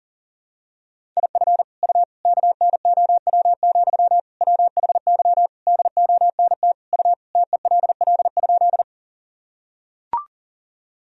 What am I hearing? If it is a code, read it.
Morse code, heard as a keyed tone, about 37 wpm, 700 Hz. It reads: IF U KNOW, WHY DONT U TELL?
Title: Sentences from Top 300 Words and Abbreviations 40wpm